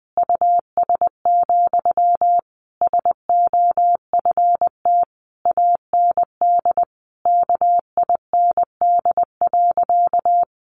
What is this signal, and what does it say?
Morse code, heard as a keyed tone, about 20 wpm, 700 Hz. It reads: US, SOFT AND KIND.
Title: Sentences from Top 1500 Words 20wpm